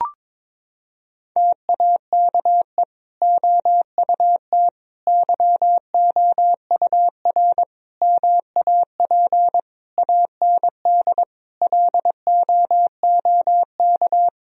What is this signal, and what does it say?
Morse code, heard as a keyed tone, about 21 wpm, 700 Hz. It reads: TAKE OUT YOUR MAP AND LOOK
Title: Sentences from Top 1000 Words - Review Part 2 - 22wpm